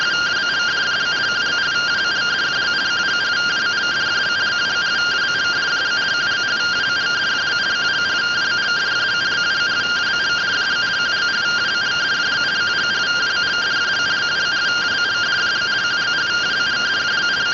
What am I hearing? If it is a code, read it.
Two-tone teleprinter signal, about 170 Hz shift.